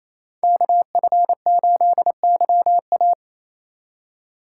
Morse code, heard as a keyed tone, about 33 wpm, 700 Hz. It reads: KF8YA
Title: Speed Racing - USA Call Signs 28wpm